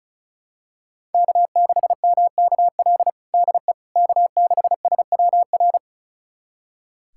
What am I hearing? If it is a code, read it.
Morse code, heard as a keyed tone, about 36 wpm, 700 Hz. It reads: K6MKL DE K6SWR
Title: USA Callsign DE Callsign - 5 Characters - 35wpm